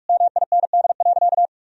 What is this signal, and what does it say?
Morse code, heard as a keyed tone, about 46 wpm, 700 Hz. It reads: MIND.
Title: Sentences from Top 1000 Words - Review Part 3 - 45wpm